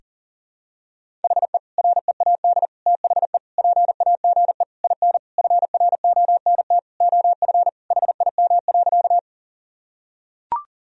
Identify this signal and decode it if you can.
Morse code, heard as a keyed tone, about 40 wpm, 700 Hz. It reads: HE READ THE PAGE IN FRONT OF HIM.
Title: Sentences from Top 1000 Words - Review Part 1 - 40wpm